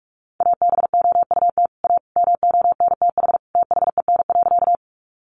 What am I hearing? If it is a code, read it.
Morse code, heard as a keyed tone, about 44 wpm, 700 Hz. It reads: ABOUT A MONTH THEN.
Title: Sentences from Top 800 Words 45wpm